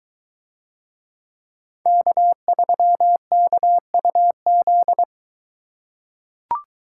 Morse code, heard as a keyed tone, about 22 wpm, 700 Hz. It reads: K3KUZ